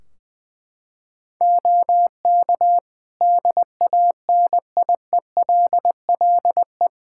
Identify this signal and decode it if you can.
Morse code, heard as a keyed tone, about 20 wpm, 700 Hz. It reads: OK DANIELLE
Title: Practice QSO - Exchange 3 - More Info - 20wpm